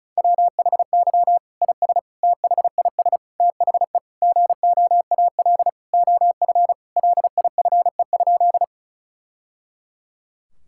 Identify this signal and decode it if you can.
Morse code, heard as a keyed tone, about 34 wpm, 700 Hz. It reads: WHY IS THIS THE GOAL OF LIFE?
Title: Sentences from Top 1000 Words - Review Part 5 - 35wpm